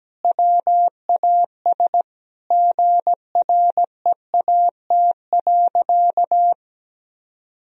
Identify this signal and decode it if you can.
Morse code, heard as a keyed tone, about 17 wpm, 700 Hz. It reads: WAS GREAT.
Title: Sentences from Top 1000 Words - Review Part 1 - 17wpm